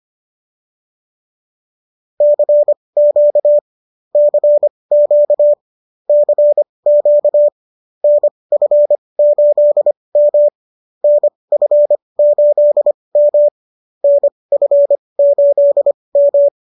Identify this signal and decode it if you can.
Morse code, heard as a keyed tone, about 25 wpm, spaced as for 20 wpm, 590 Hz. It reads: CQ CQ CQ NF8M NF8M NF8M
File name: qso221 20